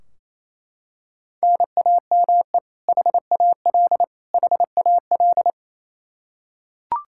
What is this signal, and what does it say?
Morse code, heard as a keyed tone, about 28 wpm, 700 Hz. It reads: NAME HAL HAL
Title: Practice QSO - Exchange 1 - Beginning - 28wpm